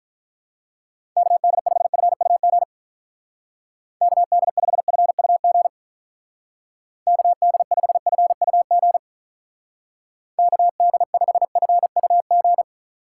Morse code, heard as a keyed tone, about 44 wpm, 700 Hz. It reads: KD5FUG KD5FUG KD5FUG KD5FUG
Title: Speed Racing - USA Call Signs 35wpm